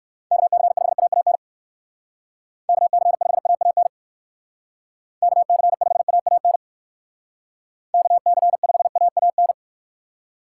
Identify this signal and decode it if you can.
Morse code, heard as a keyed tone, about 57 wpm, 700 Hz. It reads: KC5AAN KC5AAN KC5AAN KC5AAN